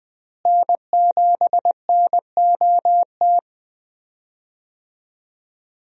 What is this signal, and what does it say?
Morse code, heard as a keyed tone, about 20 wpm, 700 Hz. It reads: N7NOT
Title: USA Call Signs - 5 Characters - Repeated 3x - 20wpm